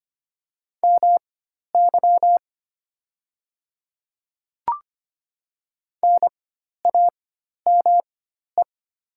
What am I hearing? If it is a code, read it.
Morse code, heard as a keyed tone, about 25 wpm, spaced as for 12 wpm, 700 Hz. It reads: MY NAME
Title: 4x Character Spacing - Top 300 Words - 25wpm